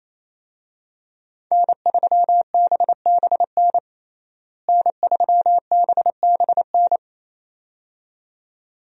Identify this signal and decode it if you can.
Morse code, heard as a keyed tone, about 28 wpm, 700 Hz. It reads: N3BBN N3BBN
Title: USA Call Signs - 5 Characters - Repeated 2x - 28wpm